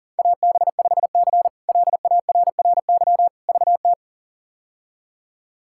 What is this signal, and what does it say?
Morse code, heard as a keyed tone, about 37 wpm, 690 Hz. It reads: AB5C LARRY VT
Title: K1USN SST QSO Practice - 40wpm